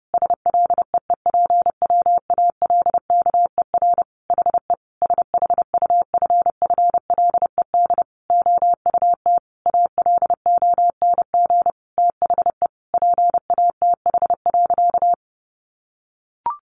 Morse code, heard as a keyed tone, about 30 wpm, 700 Hz. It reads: SLEEPWALKER HE SHUFFLED OUT ALONG THE PATH.
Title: A Sound Of Thunder 30wpm